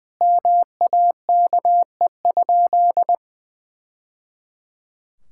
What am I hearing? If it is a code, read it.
Morse code, heard as a keyed tone, about 20 wpm, 700 Hz. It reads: MAKE?
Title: Sentences from Top 1000 Words - Review Part 6 - 20wpm